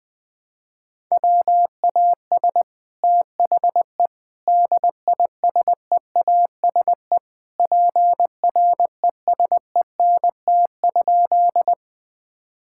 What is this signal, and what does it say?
Morse code, heard as a keyed tone, about 20 wpm, 700 Hz. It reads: WAS THE DISEASE PRESENT?
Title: Sentences from Top 1000 Words - Review Part 6 - 20wpm